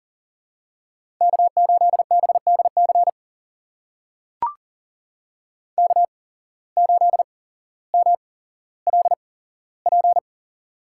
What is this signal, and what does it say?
Morse code, heard as a keyed tone, about 42 wpm, spaced as for 17 wpm, 700 Hz. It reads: K8BDC K8MLP
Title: USA Call Signs - 5 Characters - 8x Character Spacing - 40wpm